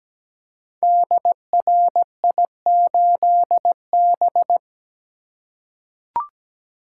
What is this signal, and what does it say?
Morse code, heard as a keyed tone, about 17 wpm, 700 Hz. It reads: DRI8B
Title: Code Groups - 5 Letter:Numbers - 17wpm